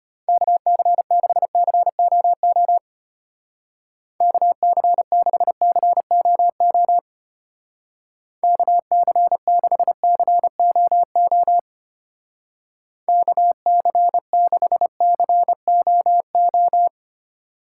Morse code, heard as a keyed tone, about 31 wpm, 700 Hz. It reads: KC6COO KC6COO KC6COO KC6COO
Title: Speed Racing - USA 2x3 Call Signs 25wpm